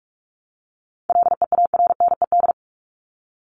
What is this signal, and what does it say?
Morse code, heard as a keyed tone, about 44 wpm, 700 Hz. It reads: LEARNED
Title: ED Suffix - Top 100 Words with Base Words 45wpm